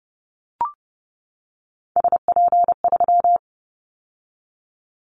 Morse code, heard as a keyed tone, about 30 wpm, 700 Hz. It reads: SP3
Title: Callsign Prefixes for Top 25 DXCC - 30wpm